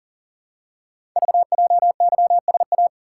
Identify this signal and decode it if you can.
Morse code, heard as a keyed tone, about 38 wpm, 700 Hz. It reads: VJYSA
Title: Code Groups - 5 Letters - 40wpm